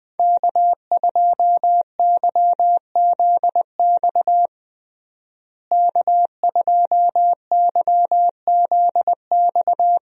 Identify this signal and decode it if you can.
Morse code, heard as a keyed tone, about 20 wpm, 700 Hz. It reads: K2YZX K2YZX
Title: USA Call Signs - 5 Characters - Repeated 3x - 20wpm